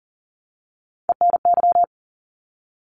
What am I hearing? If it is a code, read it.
Morse code, heard as a keyed tone, about 42 wpm, 700 Hz. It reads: ENY
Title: ARRL Sections 40wpm